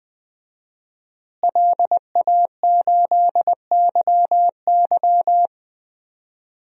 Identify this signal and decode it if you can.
Morse code, heard as a keyed tone, about 20 wpm, 700 Hz. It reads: LA8YY
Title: DXCC-100 Call Signs - 5 Characters - Repeated 3x - 20wpm